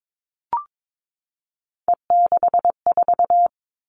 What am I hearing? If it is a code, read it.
Morse code, heard as a keyed tone, about 22 wpm, 700 Hz. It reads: E64
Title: All Cut Numbers - 3-Digit 22wpm